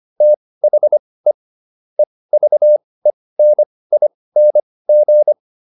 Morse code, heard as a keyed tone, about 25 wpm, spaced as for 18 wpm, 600 Hz. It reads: THE EVENING